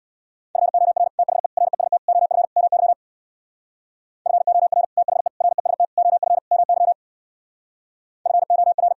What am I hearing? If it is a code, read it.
Morse code, heard as a keyed tone, about 59 wpm, 700 Hz. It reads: FOR THE REST OF MY FOR THE REST OF MY FOR
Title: Speed-Racing - Sets of 5 Words 50wpm